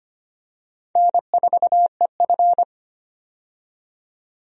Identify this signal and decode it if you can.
Morse code, heard as a keyed tone, about 25 wpm, 700 Hz. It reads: N4EF
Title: USA Call Signs - 4 Characters - 25wpm